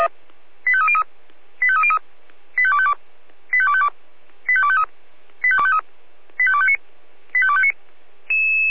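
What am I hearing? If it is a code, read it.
Five-tone selective call (ZVEI), heard as quick runs of five tones, tones of about 70 ms. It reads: 85303 85303 85252 85252 85253 85253 85259 85259
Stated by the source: Five-tone ZVEI selective call 85280, received via SDR.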